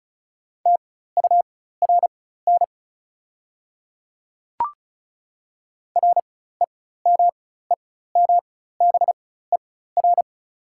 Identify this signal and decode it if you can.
Morse code, heard as a keyed tone, about 36 wpm, spaced as for 16 wpm, 700 Hz. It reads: TURN REMEMBER
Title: 4x Character Spacing - Top 500 Words - 35wpm